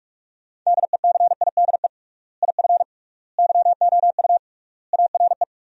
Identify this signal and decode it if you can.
Morse code, heard as a keyed tone, about 42 wpm, 700 Hz. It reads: DECIDE IF YOU ARE
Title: 3x Word-Spacing - Sentences from Top 500 Words - 45wpm